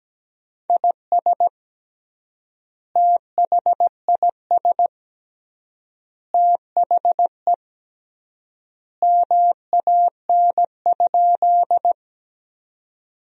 Morse code, heard as a keyed tone, about 17 wpm, 700 Hz. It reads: IS THIS THE MAN?
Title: 3x Word-Spacing - Sentences from Top 200 Words - 17wpm